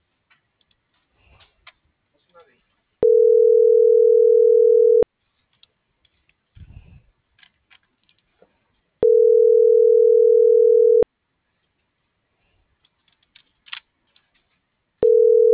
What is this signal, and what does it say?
Ringback tone (North America), 440 + 480 Hz.